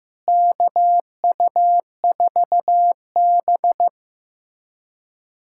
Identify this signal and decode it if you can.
Morse code, heard as a keyed tone, about 15 wpm, 700 Hz. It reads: KU4B
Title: USA Callsign DE Callsign 15wpm